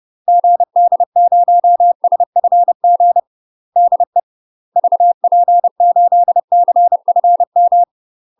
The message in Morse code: GD0SFG DE VP8CFM

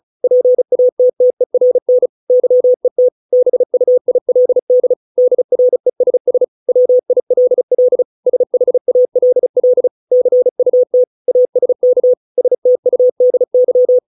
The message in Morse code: PATTERN YET BUILD DRESS WILL SHALL CUT ASK STUDY

